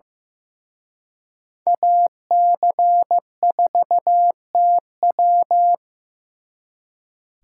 Morse code, heard as a keyed tone, about 18 wpm, 700 Hz. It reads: AC4TW
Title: Speed Racing - USA 2x2 Call Signs 15wpm